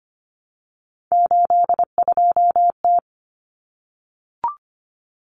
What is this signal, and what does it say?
Morse code, heard as a keyed tone, about 25 wpm, 700 Hz. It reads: 82T
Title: Common Cut Numbers - 3-Digit 25wpm